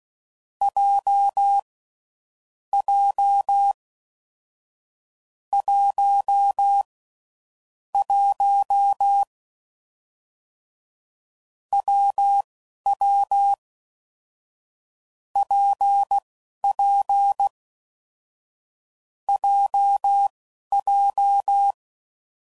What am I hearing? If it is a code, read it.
Morse code, heard as a keyed tone, about 16 wpm, spaced as for 11 wpm, 800 Hz. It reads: J J 1 1 WW PP JJ
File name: Combinations / Difficult Combos / WPJ WPJ1